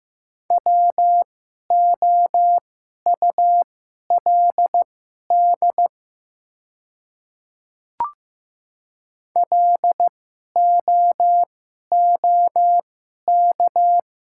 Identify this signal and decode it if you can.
Morse code, heard as a keyed tone, about 15 wpm, spaced as for 11 wpm, 700 Hz. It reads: WOULD LOOK
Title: 2x Character Spacing - Top 100 Words - 15wpm